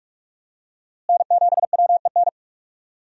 Morse code, heard as a keyed tone, about 46 wpm, 700 Hz. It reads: N7WEN
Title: USA Call Signs - 5 Characters - Repeated 3x - 45wpm